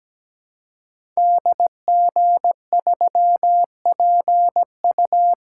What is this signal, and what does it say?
Morse code, heard as a keyed tone, about 17 wpm, 700 Hz. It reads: DG3PU
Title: Code Groups - 5 Letter:Numbers - 17wpm